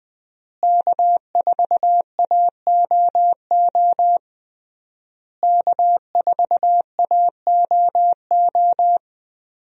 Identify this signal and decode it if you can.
Morse code, heard as a keyed tone, about 21 wpm, 700 Hz. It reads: K4AOO K4AOO
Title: USA Call Signs - 5 Characters - Repeated 3x - 20wpm